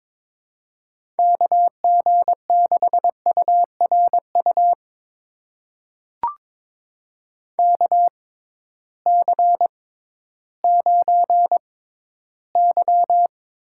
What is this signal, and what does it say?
Morse code, heard as a keyed tone, about 22 wpm, spaced as for 13 wpm, 700 Hz. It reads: KG6URU KC9Y
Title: USA Call Signs - 6 Characters - 6x Character Spacing - 22wpm